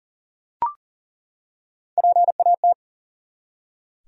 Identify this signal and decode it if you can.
Morse code, heard as a keyed tone, about 40 wpm, 700 Hz. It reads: PAT
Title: USA Common Names - Max 4 Letters - 40wpm